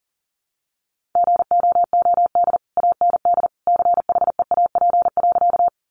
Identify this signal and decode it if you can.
Morse code, heard as a keyed tone, about 40 wpm, 700 Hz. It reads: GOOD AND CHEAP.
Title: Sentences from Top 1200 Words 40wpm